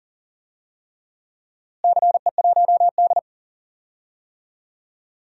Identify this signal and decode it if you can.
Morse code, heard as a keyed tone, about 40 wpm, 700 Hz. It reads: CE1D
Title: DXCC-100 Call Signs - 4 Characters - Repeated 3x - 40wpm